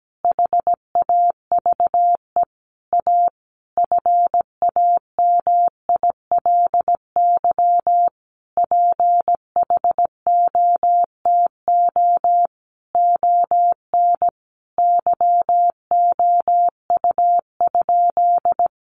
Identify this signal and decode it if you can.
Morse code, heard as a keyed tone, about 17 wpm, 700 Hz. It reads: HAVE A FAMILY PHOTO ON YOU?
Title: Sentences from Top 1200 Words 17wpm